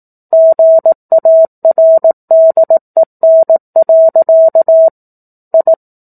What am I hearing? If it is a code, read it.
Morse code, heard as a keyed tone, about 18 wpm, 650 Hz. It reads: GARDEN. I
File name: 18 WPM